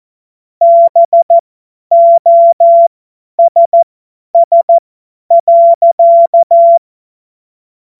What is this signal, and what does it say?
Morse code, heard as a keyed tone, about 14 wpm, spaced as for 10 wpm, 680 Hz. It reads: BOSS.